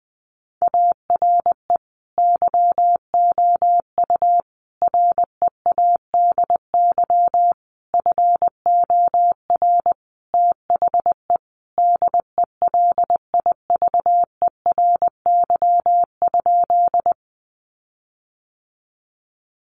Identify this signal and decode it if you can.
Morse code, heard as a keyed tone, about 19 wpm, 700 Hz. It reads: ARE YOU READY FOR THE DELIVERY?
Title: Sentences from Top 1500 Words 20wpm